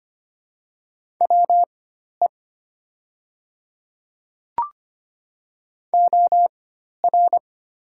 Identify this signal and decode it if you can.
Morse code, heard as a keyed tone, about 25 wpm, spaced as for 12 wpm, 700 Hz. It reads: WE OR
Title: 4x Character Spacing - Top 100 Words - 25wpm